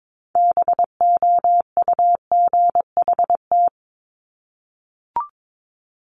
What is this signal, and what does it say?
Morse code, heard as a keyed tone, about 22 wpm, 700 Hz. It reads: BOUGHT
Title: Multi-letter Phonemes Repeated 3x 22wpm